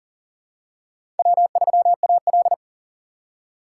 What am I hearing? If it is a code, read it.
Morse code, heard as a keyed tone, about 40 wpm, 700 Hz. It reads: W3AL
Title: USA Call Signs - 4 Characters - 6x Character Spacing - 40wpm